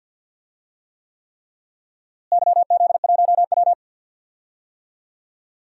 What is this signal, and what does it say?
Morse code, heard as a keyed tone, about 52 wpm, 700 Hz. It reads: YZ1W
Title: DXCC-100 Call Signs - 4 Characters - Repeated 3x - 50wpm